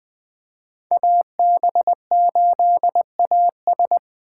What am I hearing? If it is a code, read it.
Morse code, heard as a keyed tone, about 20 wpm, 700 Hz. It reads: AB8AS
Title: 2023 POTA Activator Call Signs - 20wpm